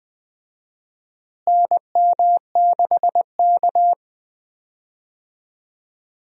Morse code, heard as a keyed tone, about 20 wpm, 700 Hz. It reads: NM6K
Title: USA Call Signs - 4 Characters - Repeated 2x - 20wpm